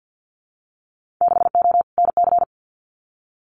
Morse code, heard as a keyed tone, about 50 wpm, 700 Hz. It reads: 6O NC
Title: Winter Field Day Exchanges 50wpm